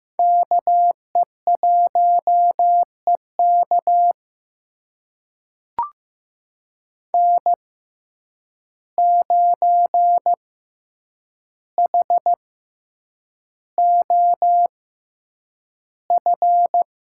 Morse code, heard as a keyed tone, about 15 wpm, spaced as for 8 wpm, 700 Hz. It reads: KE1EK N9HOF
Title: USA Call Signs - 5 Characters - 6x Character Spacing - 15wpm